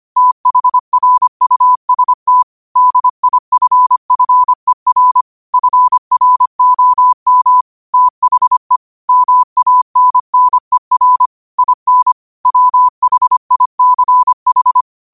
Morse code, heard as a keyed tone, about 25 wpm, 1000 Hz. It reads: THRUST DIFFER FROM THE MANNER IN WHICH